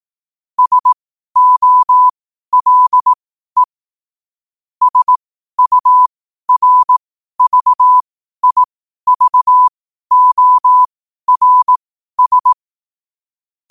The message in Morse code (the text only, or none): SOLE SURVIVORS